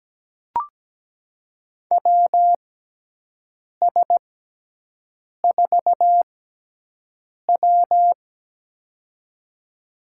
Morse code, heard as a keyed tone, about 17 wpm, spaced as for 6 wpm, 700 Hz. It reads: WS4W
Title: USA Call Signs - 6x Character Spacing - 17wpm